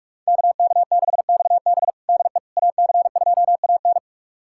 Morse code, heard as a keyed tone, about 47 wpm, 700 Hz. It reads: KK6XB DE AC2AN
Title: USA Callsign DE Callsign - 5 Characters - 45wpm